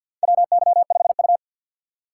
Morse code, heard as a keyed tone, about 51 wpm, 700 Hz. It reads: WY5U
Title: USA Call Signs - 4 Characters - 2x Character Spacing - 50wpm